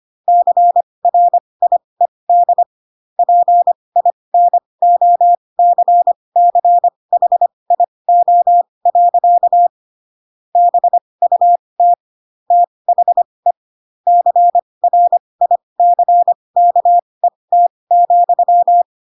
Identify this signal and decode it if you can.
Morse code, heard as a keyed tone, about 25 wpm, spaced as for 20 wpm, 700 Hz. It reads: CRIED PINOCCHIO. BUT THE CRICKET,
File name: Pinocchio / Pinocchio ch4 15 25 ik1hge